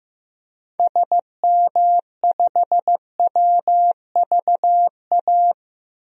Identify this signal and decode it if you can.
Morse code, heard as a keyed tone, about 15 wpm, 700 Hz. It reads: SM5WVA
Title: DXCC-100 Call Signs - 6 Characters - 15wpm